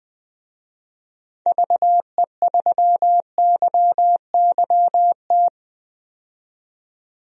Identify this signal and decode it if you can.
Morse code, heard as a keyed tone, about 20 wpm, 700 Hz. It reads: VE3YYT
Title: DXCC-100 Call Signs - 6 Characters - 20wpm